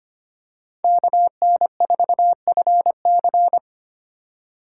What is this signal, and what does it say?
Morse code, heard as a keyed tone, about 25 wpm, 700 Hz. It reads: KN4FC
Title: USA Call Signs - 4x Character Spacing - 25wpm